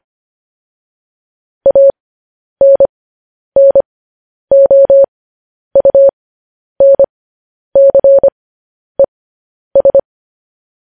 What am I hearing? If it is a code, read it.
Morse code, heard as a keyed tone, about 25 wpm, spaced as for 10 wpm, 550 Hz. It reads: ANNOUNCES
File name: podcast / News headlines-1x-25x10